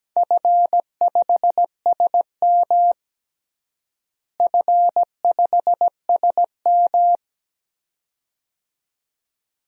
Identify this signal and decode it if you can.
Morse code, heard as a keyed tone, about 17 wpm, 700 Hz. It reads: F5SM F5SM
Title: DXCC-100 Call Signs - 4 Characters - Repeated 2x - 17wpm